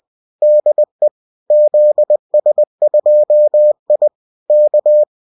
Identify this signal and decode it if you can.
Morse code, heard as a keyed tone, about 20 wpm, 600 Hz. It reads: DE ZS2I K